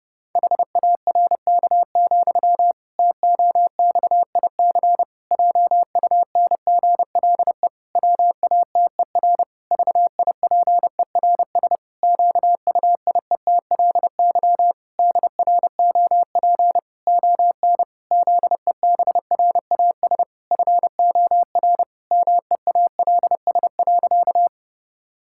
Morse code, heard as a keyed tone, about 30 wpm, 700 Hz. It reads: HARK, TOXIC JUNGLE WATER VIPERS QUIETLY DROP ON ZEBRAS FOR MEALS.
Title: Pangram Challenge v2 - 30wpm